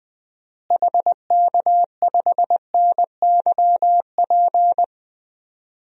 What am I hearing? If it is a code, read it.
Morse code, heard as a keyed tone, about 20 wpm, 700 Hz. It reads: HK5NYP
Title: DXCC-100 Call Signs - 6 Characters - Repeated 2x - 20wpm